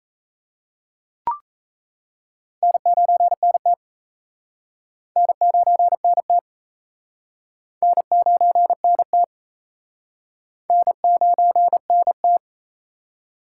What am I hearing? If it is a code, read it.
Morse code, heard as a keyed tone, about 36 wpm, 700 Hz. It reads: N9NT N9NT N9NT N9NT